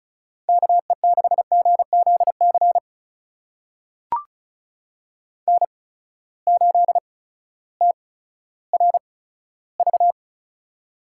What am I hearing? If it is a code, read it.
Morse code, heard as a keyed tone, about 36 wpm, spaced as for 16 wpm, 700 Hz. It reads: KE6GZC N8TRV
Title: USA Call Signs - 8x Character Spacing - 35wpm